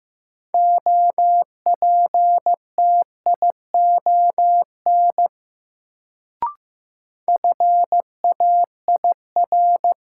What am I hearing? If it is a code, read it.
Morse code, heard as a keyed tone, about 15 wpm, 700 Hz. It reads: OPTION FAIR
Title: Sentences from Top 1300 Words - New Word Drill Alt - 15wpm